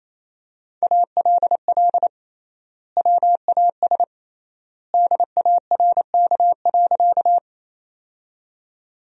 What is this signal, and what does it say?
Morse code, heard as a keyed tone, about 28 wpm, 700 Hz. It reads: ALL WAS DARK.
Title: 3x Word-Spacing - Sentences from Top 500 Words - 28wpm